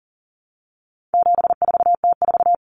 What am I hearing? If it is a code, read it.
Morse code, heard as a keyed tone, about 40 wpm, 700 Hz. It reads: 74T4
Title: Common Cut Numbers - 4-Digit 40wpm